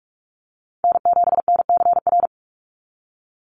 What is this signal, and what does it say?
Morse code, heard as a keyed tone, about 44 wpm, 700 Hz. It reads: N7NCR
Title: USA Call Signs 45wpm